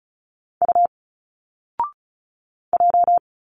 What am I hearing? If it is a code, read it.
Morse code, heard as a keyed tone, about 36 wpm, 700 Hz. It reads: U J
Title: Single Letters 35wpm